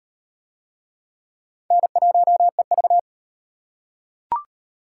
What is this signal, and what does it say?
Morse code, heard as a keyed tone, about 38 wpm, 700 Hz. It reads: N1EV